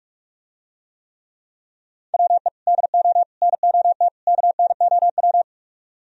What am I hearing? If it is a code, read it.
Morse code, heard as a keyed tone, about 44 wpm, 700 Hz. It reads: WE DO NOT KNOW